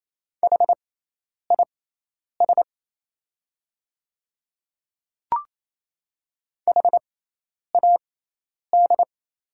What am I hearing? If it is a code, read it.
Morse code, heard as a keyed tone, about 28 wpm, spaced as for 10 wpm, 700 Hz. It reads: HIS HAD
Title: 6x Character Spacing - Top 200 Words - 28wpm